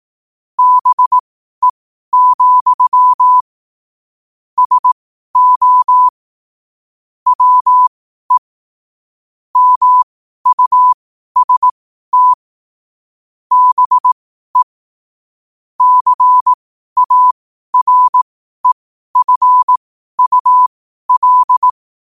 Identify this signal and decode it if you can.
Morse code, heard as a keyed tone, about 18 wpm, spaced as for 12 wpm, 1000 Hz. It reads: BE, SO WE MUST BE CAREFUL
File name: qotdfiles / 12wpm